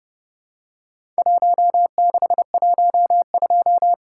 Morse code, heard as a keyed tone, about 30 wpm, 700 Hz. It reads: 1612
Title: Worked All Europe DX Contest - QTC Exchange - 30wpm